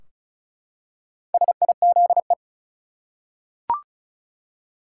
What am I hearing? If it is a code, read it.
Morse code, heard as a keyed tone, about 34 wpm, 700 Hz. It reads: SIZE
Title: Four Letter Words - 35wpm